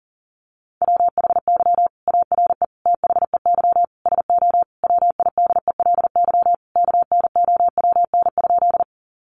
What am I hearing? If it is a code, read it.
Morse code, heard as a keyed tone, about 40 wpm, 700 Hz. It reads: WHY ARE THEY SO WIDELY KNOWN?
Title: Sentences from Top 1200 Words 40wpm